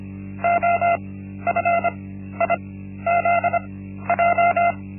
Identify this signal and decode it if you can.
Morse code, heard as a keyed tone, about 25 wpm, spaced as for 13 wpm, 690 Hz. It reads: OFIZJ
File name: Linksmakalnis-1